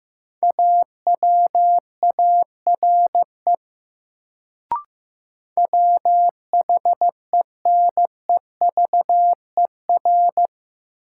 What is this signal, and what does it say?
Morse code, heard as a keyed tone, about 15 wpm, 700 Hz. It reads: AWARE WHENEVER
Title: Sentences from Top 1200 Words - New Word Drill - 15wpm